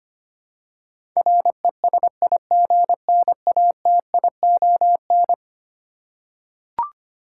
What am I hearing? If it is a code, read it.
Morse code, heard as a keyed tone, about 25 wpm, 700 Hz. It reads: RESIGNATION
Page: morse-code-ninja-practice-25wpm